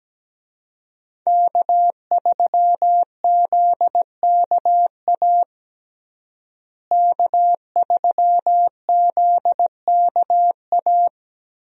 Morse code, heard as a keyed tone, about 17 wpm, 700 Hz. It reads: K3ZKA K3ZKA
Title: USA Call Signs - 5 Characters - Repeated 2x - 17wpm